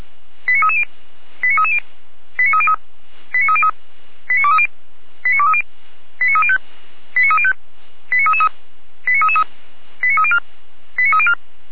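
Five-tone selective call (ZVEI), heard as quick runs of five tones, tones of about 80 ms. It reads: 89330 89330 89303 89303 89230 89230 89305 89305 89333 89304 89304
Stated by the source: Received on an SDR, ZVEI five-tone selective call, from the Feuerwehr Orlamünde (fire brigade) audio archive.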